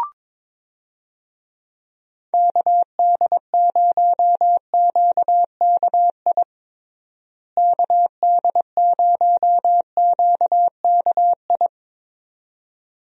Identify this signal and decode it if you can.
Morse code, heard as a keyed tone, about 22 wpm, 700 Hz. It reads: KD0QKI KD0QKI
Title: USA Call Signs - 6 Characters - Repeated 2x - 22wpm